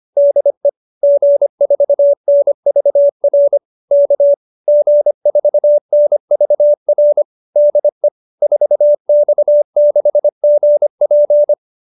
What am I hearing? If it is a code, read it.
Morse code, heard as a keyed tone, about 25 wpm, 570 Hz. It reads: DE G4NVR K G4NVR DE 4X6GP
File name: QSO315 25